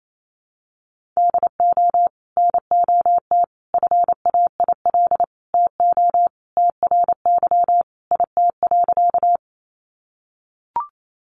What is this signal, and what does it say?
Morse code, heard as a keyed tone, about 28 wpm, 700 Hz. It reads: DO NOT FAIL TO TRY IT.
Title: Sentences from Top 1100 Words 28wpm